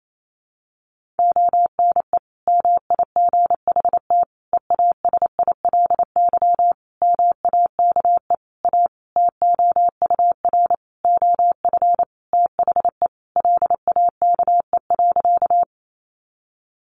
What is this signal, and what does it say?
Morse code, heard as a keyed tone, about 28 wpm, 700 Hz. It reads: ONE MIGHT EASILY MAKE A TOUR OF THE LAKE.
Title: Sentences from Top 1400 Words 28wpm